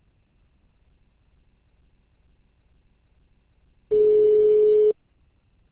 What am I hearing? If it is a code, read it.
Ringback tone (Europe, 425 Hz).